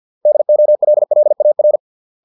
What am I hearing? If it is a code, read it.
Morse code, heard as a keyed tone, about 50 wpm, 600 Hz. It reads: DOLLAR